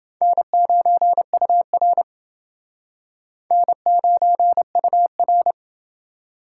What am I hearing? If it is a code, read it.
Morse code, heard as a keyed tone, about 28 wpm, 700 Hz. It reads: N9UR N9UR